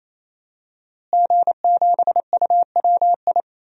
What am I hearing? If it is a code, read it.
Morse code, heard as a keyed tone, about 27 wpm, 700 Hz. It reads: G7UWI
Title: DXCC-100 Call Signs - 5 Characters - Repeated 3x - 28wpm